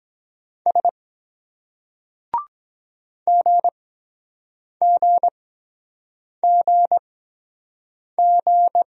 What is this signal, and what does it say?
Morse code, heard as a keyed tone, about 23 wpm, spaced as for 7 wpm, 700 Hz. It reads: S GGGG